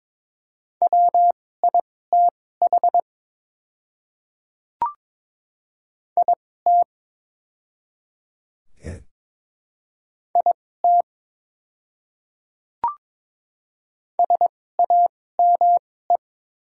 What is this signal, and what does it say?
Morse code, heard as a keyed tone, about 22 wpm, spaced as for 16 wpm, 700 Hz. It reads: WITH IT IT SAME
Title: 2x Character Spacing - Top 200 Words - 22wpm